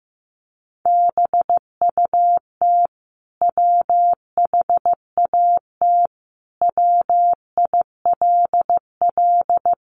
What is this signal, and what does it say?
Morse code, heard as a keyed tone, about 15 wpm, 700 Hz. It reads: BUT WHAT WILL
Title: Sentences from Top 100 Words 15wpm